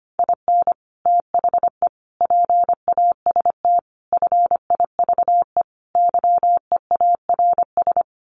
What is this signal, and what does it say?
Morse code, heard as a keyed tone, about 25 wpm, 700 Hz. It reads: IN THE PAST FIVE YEARS
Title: Sets of 5 Words Encore 25wpm